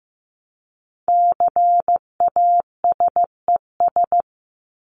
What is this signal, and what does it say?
Morse code, heard as a keyed tone, about 15 wpm, 700 Hz. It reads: CASES
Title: ES:S Suffix - Top 100 Words with Base Words 15wpm